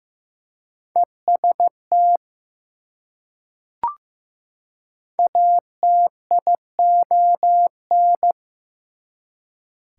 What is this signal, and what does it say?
Morse code, heard as a keyed tone, about 15 wpm, 700 Hz. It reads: EST ATION
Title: Prefixes and Suffixes 15wpm